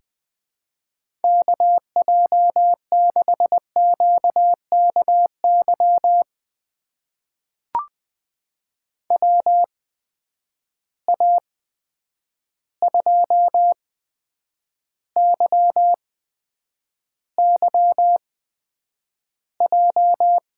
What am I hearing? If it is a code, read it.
Morse code, heard as a keyed tone, about 20 wpm, spaced as for 9 wpm, 700 Hz. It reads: KJ6QKY WA2YYJ
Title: USA Call Signs - 8x Character Spacing - 20wpm